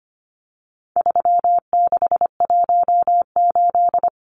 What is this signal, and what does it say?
Morse code, heard as a keyed tone, about 25 wpm, 700 Hz. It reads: 3618
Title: Numbers - 4-Digit 25wpm